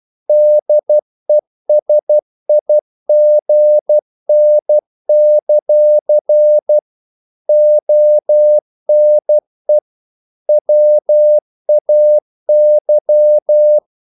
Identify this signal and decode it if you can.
Morse code, heard as a keyed tone, about 12 wpm, 600 Hz. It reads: DESIGN; ONE WAY
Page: English quotations in CW (telegraphy - morse)